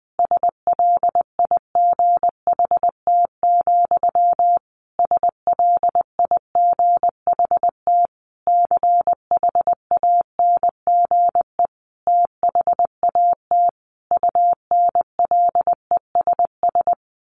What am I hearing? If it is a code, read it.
Morse code, heard as a keyed tone, about 20 wpm, 700 Hz. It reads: SLIGHT, SLIGHT CHANGE THAT UNLESS
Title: A Sound Of Thunder 20wpm